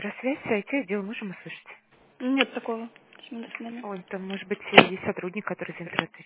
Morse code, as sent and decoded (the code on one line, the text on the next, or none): none